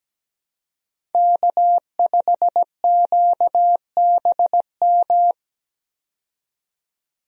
Morse code, heard as a keyed tone, about 17 wpm, 700 Hz. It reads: K5QBM
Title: USA Call Signs - 5 Characters - 17wpm